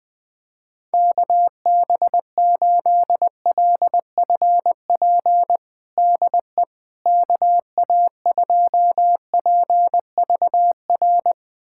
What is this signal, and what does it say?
Morse code, heard as a keyed tone, about 20 wpm, 700 Hz. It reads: KB8LFP DE KA2PVR
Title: USA Callsign DE Callsign - 6 Characters - 20wpm